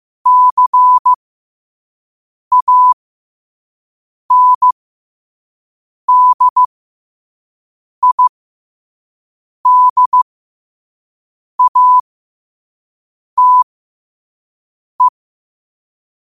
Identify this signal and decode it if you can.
Morse code, heard as a keyed tone, about 15 wpm, spaced as for 5 wpm, 1000 Hz. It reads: CANDIDATE
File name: 5wpm